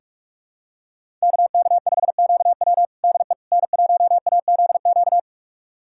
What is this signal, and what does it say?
Morse code, heard as a keyed tone, about 43 wpm, 700 Hz. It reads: KK5QW DE N1AZQ
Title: USA Callsign DE Callsign - 5 Characters - 45wpm